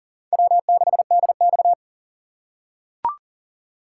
Morse code, heard as a keyed tone, about 40 wpm, 700 Hz. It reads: W6DX
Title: USA Call Signs - 2x Character Spacing - 40wpm